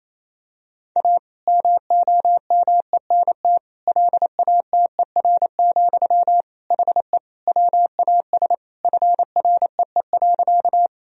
Morse code, heard as a keyed tone, about 28 wpm, 700 Hz. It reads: A MOMENT LATER, HE WAS FREE.
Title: Sentences from Top 1000 Words - Review Part 4 - 28wpm